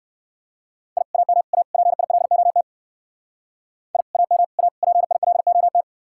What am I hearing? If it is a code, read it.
Morse code, heard as a keyed tone, about 55 wpm, 700 Hz. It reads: I AM A PILOT I AM A PILOT
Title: Speed-Racing Common QSO Phrases - 45wpm